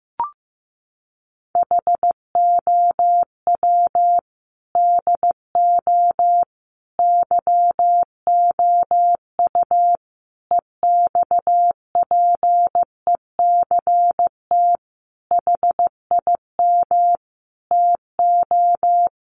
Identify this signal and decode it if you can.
Morse code, heard as a keyed tone, about 15 wpm, 700 Hz. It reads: HOW DO YOU EXPECT HIM TO
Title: Sentences from Top 1100 Words 15wpm